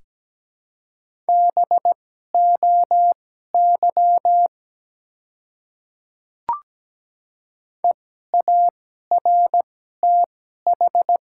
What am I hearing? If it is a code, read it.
Morse code, heard as a keyed tone, about 17 wpm, spaced as for 12 wpm, 700 Hz. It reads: BOY EARTH
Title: 2x Character Spacing - Top 200 Words - 17wpm